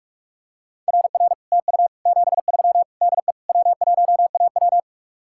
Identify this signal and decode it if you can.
Morse code, heard as a keyed tone, about 50 wpm, 700 Hz. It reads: RR TU 73 DE W1AW